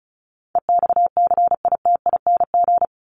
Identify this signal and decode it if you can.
Morse code, heard as a keyed tone, about 34 wpm, 700 Hz. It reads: EXCITING
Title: Top 1751-2000 Words 35wpm